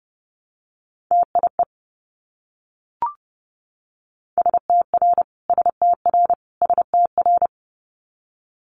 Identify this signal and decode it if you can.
Morse code, heard as a keyed tone, about 30 wpm, 700 Hz. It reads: TIE STR STR STR
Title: Multi-letter Phonemes Repeated 3x 30wpm